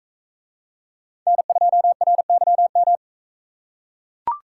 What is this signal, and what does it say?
Morse code, heard as a keyed tone, about 41 wpm, 700 Hz. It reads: N2RYM